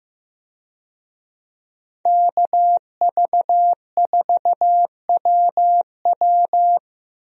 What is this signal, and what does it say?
Morse code, heard as a keyed tone, about 15 wpm, 700 Hz. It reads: KV4WW
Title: USA Call Signs - 5 Characters - Repeated 2x - 15wpm